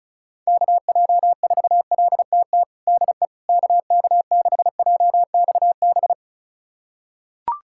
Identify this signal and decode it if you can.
Morse code, heard as a keyed tone, about 34 wpm, 700 Hz. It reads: KJ4LTT DE KK6JXB
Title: USA Callsign DE Callsign 35wpm